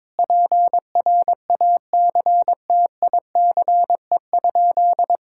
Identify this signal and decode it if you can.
Morse code, heard as a keyed tone, about 22 wpm, 700 Hz. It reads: PRACTICE?
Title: Sentences from Top 1000 Words - Review Part 5 - 22wpm